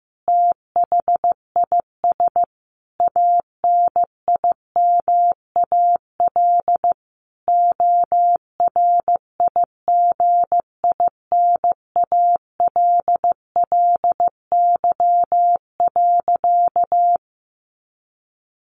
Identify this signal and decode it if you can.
Morse code, heard as a keyed tone, about 15 wpm, 700 Hz. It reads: THIS ANIMAL ORIGINALLY.
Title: A Sound Of Thunder 15wpm